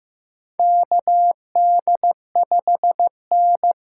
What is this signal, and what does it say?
Morse code, heard as a keyed tone, about 15 wpm, 700 Hz. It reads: KD5N
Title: USA Call Signs - 6 Characters - 8x Character Spacing - 15wpm